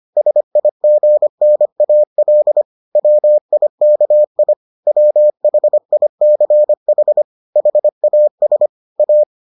Morse code, heard as a keyed tone, about 25 wpm, 600 Hz. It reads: SIGNAL WIKI WHICH HAS A